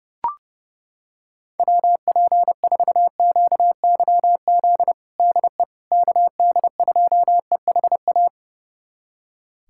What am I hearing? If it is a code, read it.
Morse code, heard as a keyed tone, about 32 wpm, 700 Hz. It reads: WP4QYZ DE KD2EHA
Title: USA Callsign DE Callsign - 6 Characters - 30wpm